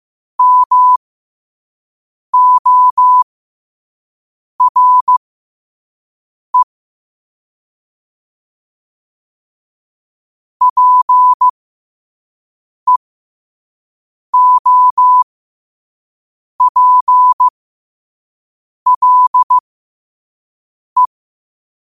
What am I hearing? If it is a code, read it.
Morse code, heard as a keyed tone, about 15 wpm, spaced as for 5 wpm, 1000 Hz. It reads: MORE PEOPLE
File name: qotdfiles / 5wpm